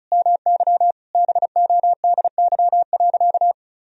Morse code, heard as a keyed tone, about 36 wpm, 700 Hz. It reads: MY BODY.
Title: Sentences from Top 1000 Words - Review Part 1 - 35wpm